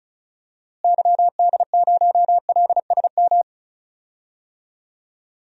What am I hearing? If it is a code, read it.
Morse code, heard as a keyed tone, about 36 wpm, 700 Hz. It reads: YD0LSM
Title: DXCC-100 Call Signs - 6 Characters - Repeated 3x - 35wpm